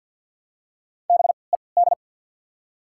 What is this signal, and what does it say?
Morse code, heard as a keyed tone, about 50 wpm, spaced as for 28 wpm, 690 Hz. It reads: BED
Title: 3x Character Spacing - Top 500 Words - 50wpm